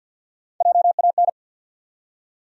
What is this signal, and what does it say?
Morse code, heard as a keyed tone, about 49 wpm, 700 Hz. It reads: JAN
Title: USA Common Names - Max 3 Letters - 50wpm